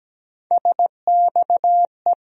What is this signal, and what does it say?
Morse code, heard as a keyed tone, about 17 wpm, 700 Hz. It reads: SXE
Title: USA Call Sign Postfixes - 3 Letters 17wpm